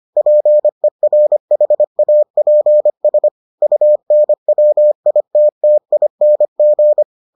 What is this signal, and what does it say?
Morse code, heard as a keyed tone, about 25 wpm, 600 Hz. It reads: PERHAPS UNWITTING